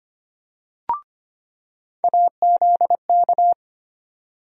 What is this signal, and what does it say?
Morse code, heard as a keyed tone, about 25 wpm, 700 Hz. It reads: AZK
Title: USA Call Sign Postfixes - 3 Letters 25wpm